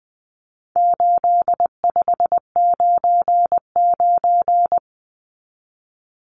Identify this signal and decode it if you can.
Morse code, heard as a keyed tone, about 20 wpm, 700 Hz. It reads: 8599
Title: Numbers - 4-Digit 20wpm